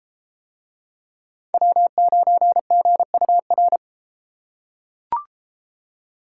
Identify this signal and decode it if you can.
Morse code, heard as a keyed tone, about 32 wpm, 700 Hz. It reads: W9GUR